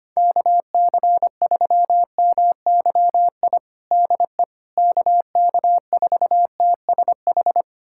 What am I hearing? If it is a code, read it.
Morse code, heard as a keyed tone, about 25 wpm, 700 Hz. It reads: KC3MYI DE KK4TSH
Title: USA Callsign DE Callsign - 6 Characters - 25wpm